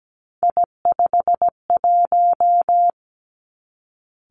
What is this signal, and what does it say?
Morse code, heard as a keyed tone, about 17 wpm, 700 Hz. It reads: I51
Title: Three Challenging Letter-Number Combinations 17wpm